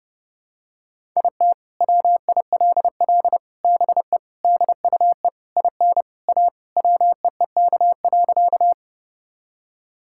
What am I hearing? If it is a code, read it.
Morse code, heard as a keyed tone, about 30 wpm, 700 Hz. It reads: IT WILL BE DUE IN A WEEK.
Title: Sentences from Top 1000 Words - Review Part 6 - 30wpm